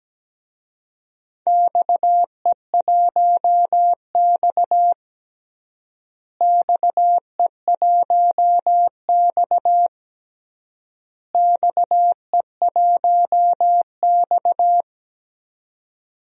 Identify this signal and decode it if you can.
Morse code, heard as a keyed tone, about 17 wpm, 700 Hz. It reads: XE1X XE1X XE1X
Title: DXCC-100 Call Signs - 4 Characters - Repeated 3x - 17wpm